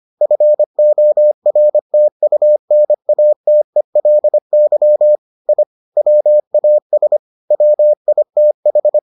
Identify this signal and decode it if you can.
Morse code, heard as a keyed tone, about 25 wpm, 600 Hz. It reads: FORTUNATELY I WAS WITH